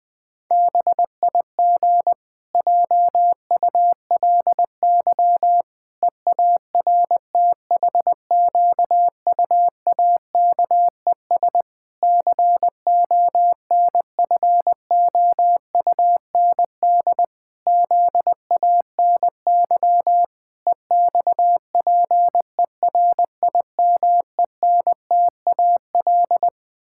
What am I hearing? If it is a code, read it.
Morse code, heard as a keyed tone, about 20 wpm, 700 Hz. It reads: BIG JULY EARTHQUAKES CONFOUND ZANY EXPERIMENTAL
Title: Pangram Challenge v2 - 20wpm